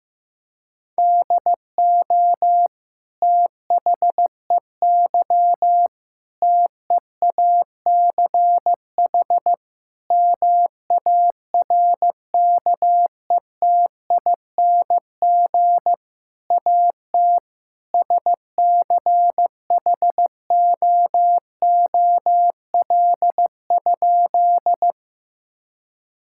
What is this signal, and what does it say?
Morse code, heard as a keyed tone, about 15 wpm, 700 Hz. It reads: DO THEY TEACH MARKETING AT SCHOOL?
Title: Sentences from Top 1000 Words - Review Part 5 - 15wpm